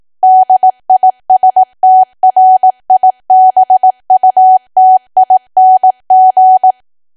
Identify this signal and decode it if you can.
Morse code, heard as a keyed tone, about 18 wpm, 750 Hz. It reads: DISTRIBUTING
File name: cwa / W1AW-Level2 / 18WPM